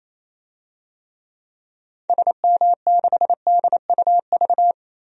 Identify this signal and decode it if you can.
Morse code, heard as a keyed tone, about 28 wpm, 700 Hz. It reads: SM6DUV
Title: DXCC-100 Call Signs - 6 Characters - 28wpm